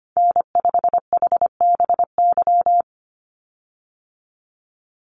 Morse code, heard as a keyed tone, about 25 wpm, 700 Hz. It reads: N5HBY
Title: USA Call Signs 25wpm